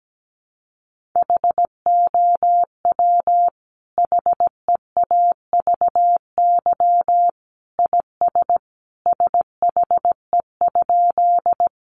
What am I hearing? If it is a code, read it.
Morse code, heard as a keyed tone, about 17 wpm, 700 Hz. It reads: HOW HEAVY IS SHE?
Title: Sentences from Top 1100 Words 17wpm